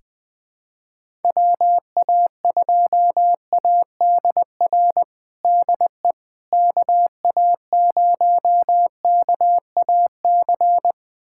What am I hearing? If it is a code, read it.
Morse code, heard as a keyed tone, about 20 wpm, 700 Hz. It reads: WA2ADR DE KA0KAC
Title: USA Callsign DE Callsign - 6 Characters - 20wpm